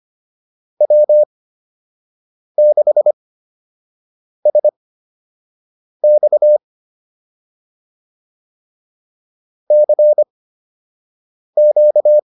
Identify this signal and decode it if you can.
Morse code, heard as a keyed tone, about 25 wpm, spaced as for 6 wpm, 600 Hz. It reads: W6SX CQ
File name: QSO 102 06